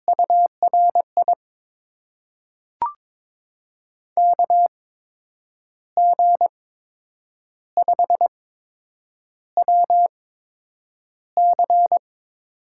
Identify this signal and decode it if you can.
Morse code, heard as a keyed tone, about 21 wpm, spaced as for 8 wpm, 700 Hz. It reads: URI KG5WC
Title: USA Call Signs - 6 Characters - 8x Character Spacing - 22wpm